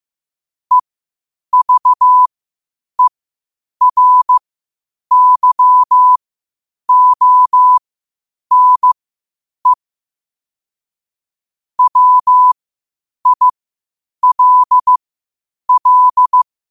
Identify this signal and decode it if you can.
Morse code, heard as a keyed tone, about 15 wpm, spaced as for 8 wpm, 1000 Hz. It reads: EVERYONE WILL
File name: qotdfiles / 8wpm